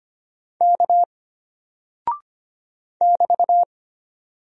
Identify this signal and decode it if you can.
Morse code, heard as a keyed tone, about 25 wpm, 700 Hz. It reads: K <BT>
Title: Single Character (Letter:Number:Punctuation:Prosign) Alt 25wpm